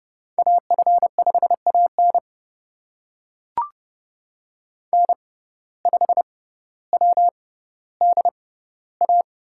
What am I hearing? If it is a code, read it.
Morse code, heard as a keyed tone, about 30 wpm, spaced as for 15 wpm, 700 Hz. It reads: AF5AN N5WDA
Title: USA Call Signs - 5 Characters - 6x Character Spacing - 30wpm